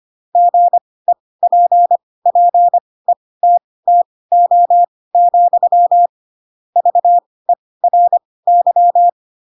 Morse code, heard as a keyed tone, about 25 wpm, spaced as for 18 wpm, 700 Hz. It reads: GEPPETTO, VERY
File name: Pinocchio / Pinocchio ch7 12 25 ik1hge